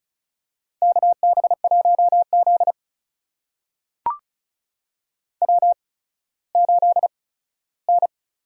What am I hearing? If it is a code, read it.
Morse code, heard as a keyed tone, about 36 wpm, spaced as for 17 wpm, 700 Hz. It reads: KB1Z W8N
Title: USA Call Signs - 4 Characters - 8x Character Spacing - 35wpm